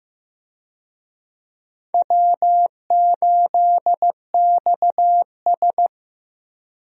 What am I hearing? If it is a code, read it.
Morse code, heard as a keyed tone, about 15 wpm, 700 Hz. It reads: W8XS
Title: USA Call Signs - 4 Characters - Repeated 3x - 15wpm